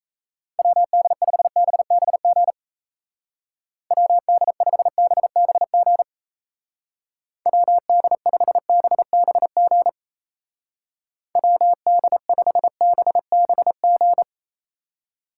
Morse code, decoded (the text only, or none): WD5BBG WD5BBG WD5BBG WD5BBG